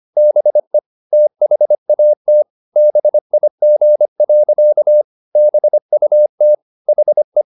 Morse code, heard as a keyed tone, about 25 wpm, 600 Hz. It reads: BE THAT BIG. BUT HE